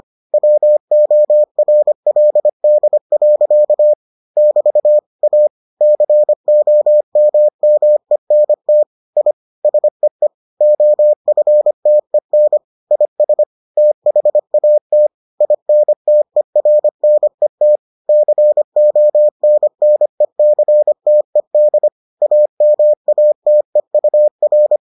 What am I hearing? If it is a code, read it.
Morse code, heard as a keyed tone, about 25 wpm, 600 Hz. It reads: WORLD. <BT> A COMMENT I SEE OFTEN IS THAT INTERNET CONNECTED AMATEUR